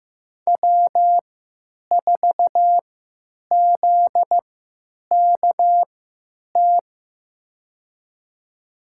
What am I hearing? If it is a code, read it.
Morse code, heard as a keyed tone, about 15 wpm, spaced as for 9 wpm, 700 Hz. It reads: W4ZKT
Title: USA Call Signs - 5 Characters - 3x Character Spacing - 15wpm